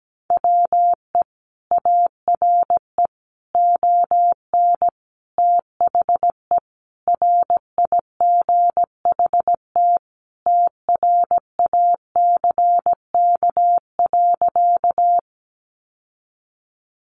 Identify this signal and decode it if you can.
Morse code, heard as a keyed tone, about 17 wpm, 700 Hz. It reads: WE ARE ON THE RIGHT TRACK.
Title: Sentences from Top 1100 Words 17wpm